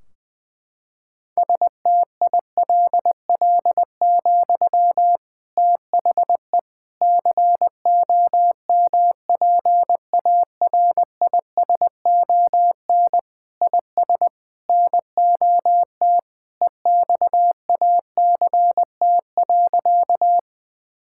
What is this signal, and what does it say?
Morse code, heard as a keyed tone, about 20 wpm, 700 Hz. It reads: STILL, THE COMPARISON IS NOT EXACT.
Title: Sentences from Top 1600 Words 20wpm